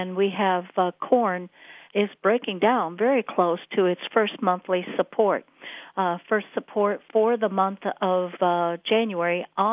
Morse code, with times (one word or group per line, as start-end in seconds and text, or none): none